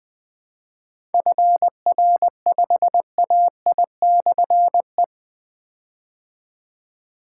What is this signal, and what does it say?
Morse code, heard as a keyed tone, about 20 wpm, 700 Hz. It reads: FR5AI/E
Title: Difficult Call Signs 20wpm